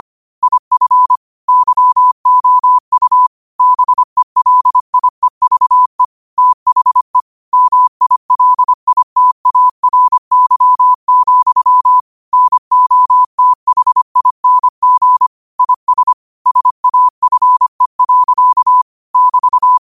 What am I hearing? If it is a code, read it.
Morse code, heard as a keyed tone, about 25 wpm, 1000 Hz. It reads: IF YOU BELIEVE THE MILITARY, NOTHING IS SAFE. <BT>
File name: qotdfiles / 25wpm